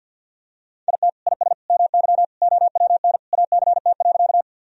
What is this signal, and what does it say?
Morse code, heard as a keyed tone, about 53 wpm, 700 Hz. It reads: IT IS MY OWN ACT.
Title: Sentences from Top 1000 Words - Review Part 1 - 50wpm